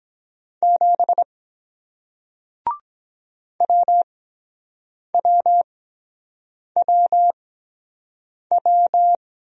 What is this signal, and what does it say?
Morse code, heard as a keyed tone, about 22 wpm, spaced as for 7 wpm, 700 Hz. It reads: 7 WWWW